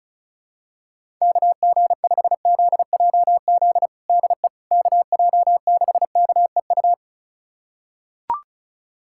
Morse code, heard as a keyed tone, about 34 wpm, 700 Hz. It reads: KG5ZJZ DE KJ6KEU
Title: USA Callsign DE Callsign - 6 Characters - 35wpm